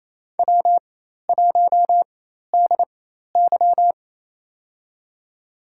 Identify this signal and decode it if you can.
Morse code, heard as a keyed tone, about 28 wpm, spaced as for 13 wpm, 700 Hz. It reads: W1DY
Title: USA Call Signs - 4 Characters - 4x Character Spacing - 28wpm